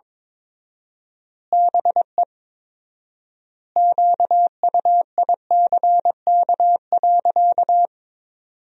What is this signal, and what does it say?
Morse code, heard as a keyed tone, about 22 wpm, 700 Hz. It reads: BE QUICK.
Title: 4x Word-Spacing - Sentences from Top 500 Words - 22wpm